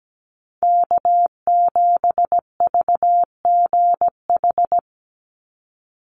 Morse code, heard as a keyed tone, about 17 wpm, 700 Hz. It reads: K7VGH
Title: USA Call Signs 17wpm